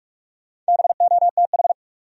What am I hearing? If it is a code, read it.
Morse code, heard as a keyed tone, about 54 wpm, 700 Hz. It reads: BOTH
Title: Speed-Racing Top 500 Words 45wpm